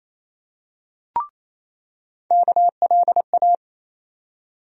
Morse code, heard as a keyed tone, about 28 wpm, 700 Hz. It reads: KLA
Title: USA Call Sign Postfixes - 3 Letters 28wpm